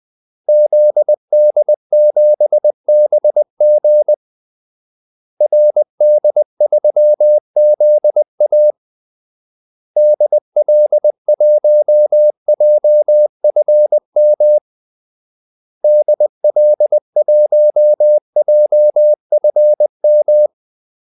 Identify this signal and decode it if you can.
Morse code, heard as a keyed tone, about 20 wpm, 600 Hz. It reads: ZD7BG RD3ZA DL1JFM DL1JFM
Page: Kaldesignaler 20 wpm